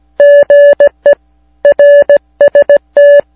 Morse code, heard as a keyed tone, about 17 wpm, 580 Hz. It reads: GE RST